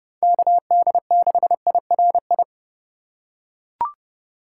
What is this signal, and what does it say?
Morse code, heard as a keyed tone, about 30 wpm, 700 Hz. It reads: KD6IRI
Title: K1USN SST QSO Practice - 30wpm